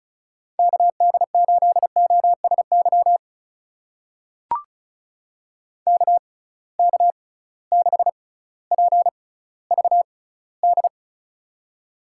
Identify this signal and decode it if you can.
Morse code, heard as a keyed tone, about 36 wpm, spaced as for 18 wpm, 700 Hz. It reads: KD8OSY KK6PVD
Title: USA Call Signs - 6x Character Spacing - 35wpm